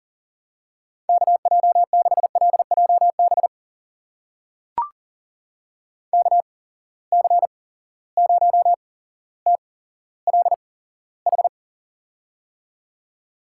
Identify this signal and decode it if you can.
Morse code, heard as a keyed tone, about 41 wpm, spaced as for 17 wpm, 700 Hz. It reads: KJ6LJB KC0TLH
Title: USA Call Signs - 6 Characters - 8x Character Spacing - 40wpm